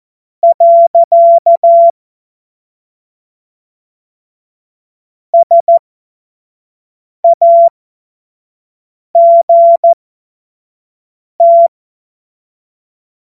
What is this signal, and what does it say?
Morse code, heard as a keyed tone, about 14 wpm, spaced as for 5 wpm, 680 Hz. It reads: . SAGT
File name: Morsekurs / Witze / Witz 30. Dezember 2025 25 F70